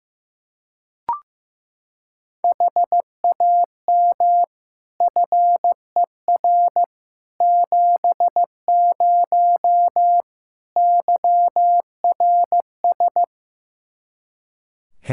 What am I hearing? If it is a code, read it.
Morse code, heard as a keyed tone, about 15 wpm, 700 Hz. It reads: HAM FER 70 YRS
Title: Practice QSO - Exchange 3 - More Info - 15wpm